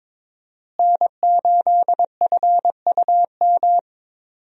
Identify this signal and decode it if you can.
Morse code, heard as a keyed tone, about 21 wpm, 700 Hz. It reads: N8FUM
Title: USA Call Signs - 5 Characters - Repeated 2x - 22wpm